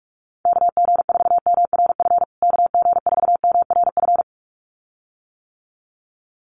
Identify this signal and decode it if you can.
Morse code, heard as a keyed tone, about 44 wpm, 700 Hz. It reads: KG4MRF KG4MRF
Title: USA Call Signs Repeated 2x 45wpm